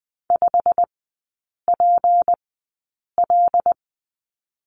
Morse code, heard as a keyed tone, about 20 wpm, spaced as for 8 wpm, 690 Hz. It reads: 5PL
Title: 159 Lesson 26e - Letter g - Three Characters 20wpm